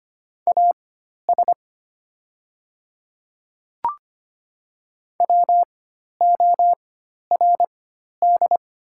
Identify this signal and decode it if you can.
Morse code, heard as a keyed tone, about 25 wpm, spaced as for 12 wpm, 690 Hz. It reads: AS WORD
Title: 4x Character Spacing - Top 100 Words - 25wpm